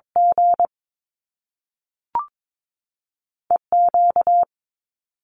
Morse code, heard as a keyed tone, about 22 wpm, spaced as for 3 wpm, 700 Hz. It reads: GE Q
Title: Top Two Letter Combinations in Words 22wpm